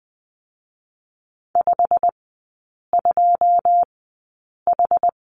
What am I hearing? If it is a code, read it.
Morse code, heard as a keyed tone, about 20 wpm, spaced as for 8 wpm, 700 Hz. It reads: 52H
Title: 100 Lesson 17e - Letter u - Three Characters 20wpm